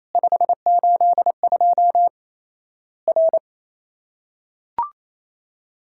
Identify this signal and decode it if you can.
Morse code, heard as a keyed tone, about 28 wpm, 700 Hz. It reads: 582 R
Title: Worked All Europe DX Contest - QTC Exchange - 28wpm